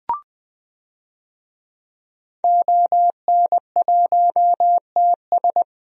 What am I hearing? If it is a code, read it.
Morse code, heard as a keyed tone, about 20 wpm, 700 Hz. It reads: ON1TS
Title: DXCC-100 Call Signs - 5 Characters - Repeated 2x - 20wpm